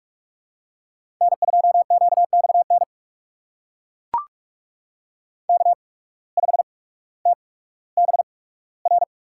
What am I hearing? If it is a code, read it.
Morse code, heard as a keyed tone, about 42 wpm, spaced as for 19 wpm, 700 Hz. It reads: N2QXN K5TBR
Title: USA Call Signs - 5 Characters - 8x Character Spacing - 45wpm